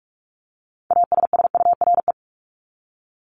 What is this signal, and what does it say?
Morse code, heard as a keyed tone, about 44 wpm, 700 Hz. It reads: ASSURE
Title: Top 1501-1750 Words 45wpm